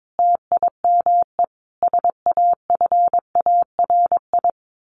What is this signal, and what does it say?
Morse code, heard as a keyed tone, about 22 wpm, 700 Hz. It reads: TIME SAFARI
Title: A Sound Of Thunder 22wpm